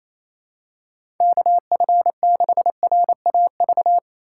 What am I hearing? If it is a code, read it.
Morse code, heard as a keyed tone, about 28 wpm, 700 Hz. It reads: KF6RAV
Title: USA Call Signs - 3x Character Spacing - 28wpm